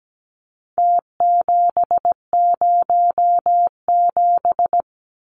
Morse code, heard as a keyed tone, about 17 wpm, 700 Hz. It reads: T707
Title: Numbers - Contest Style - 4-Digit 17wpm